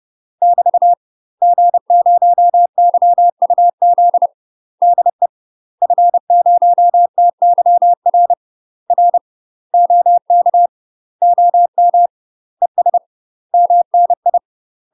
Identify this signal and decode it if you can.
Morse code, heard as a keyed tone, about 30 wpm, 700 Hz. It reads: <BT> G0YUZ DE F0TYR R OK OM ES MNI
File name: UK MASTER QSOs TRAINING / 30WPM / UK MASTER QSOs 0047